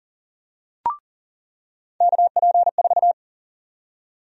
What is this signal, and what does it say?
Morse code, heard as a keyed tone, about 40 wpm, 700 Hz. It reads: KP4
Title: USA Call Sign Prefixes - 40wpm